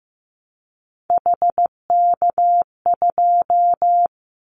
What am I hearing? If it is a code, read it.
Morse code, heard as a keyed tone, about 15 wpm, 700 Hz. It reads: HK2
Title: Callsign Prefixes for Top 100 DXCC - 15wpm